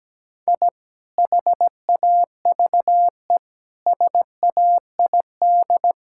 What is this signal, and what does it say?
Morse code, heard as a keyed tone, about 17 wpm, 700 Hz. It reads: I HAVE SAID
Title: Sentences from Top 1000 Words - Review Part 6 - 17wpm